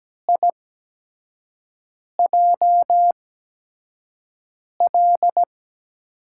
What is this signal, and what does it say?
Morse code, heard as a keyed tone, about 17 wpm, spaced as for 5 wpm, 700 Hz. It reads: IJL
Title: USA Call Signs - 5 Characters - 8x Character Spacing - 17wpm